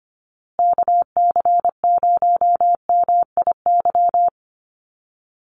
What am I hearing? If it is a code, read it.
Morse code, heard as a keyed tone, about 24 wpm, 700 Hz. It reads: KC0MIY
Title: USA Call Signs Repeated 3x 25wpm